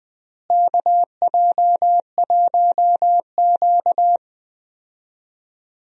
Morse code, heard as a keyed tone, about 20 wpm, 700 Hz. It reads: KJ1Q
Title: USA Call Signs - 4 Characters - Repeated 2x - 20wpm